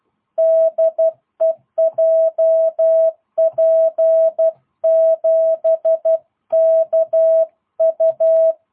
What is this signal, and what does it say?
Morse code, heard as a keyed tone, about 12 wpm, 650 Hz. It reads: DE JP7KU
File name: jh8cbh / 2022cw